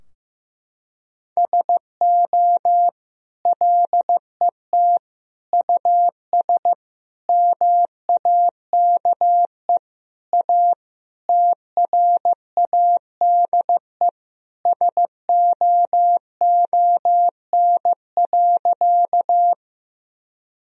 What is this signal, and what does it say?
Morse code, heard as a keyed tone, about 15 wpm, 700 Hz. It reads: SO LET US MAKE A TRADE SOON.
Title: Sentences from Top 1000 Words - Review Part 4 - 15wpm